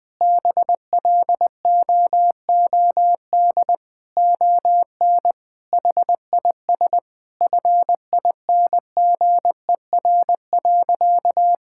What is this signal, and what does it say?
Morse code, heard as a keyed tone, about 20 wpm, 700 Hz. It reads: BLOOD ON HIS FINGER.
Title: Sentences from Top 1600 Words 20wpm